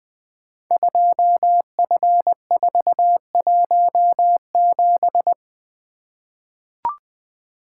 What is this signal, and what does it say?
Morse code, heard as a keyed tone, about 20 wpm, 700 Hz. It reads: 2F417
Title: Code Groups - 5 Letter:Numbers - 20wpm